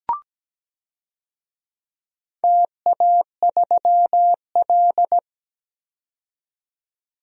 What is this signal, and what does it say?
Morse code, heard as a keyed tone, about 17 wpm, 700 Hz. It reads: TA3L
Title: DXCC-100 Call Signs - 4 Characters - 17wpm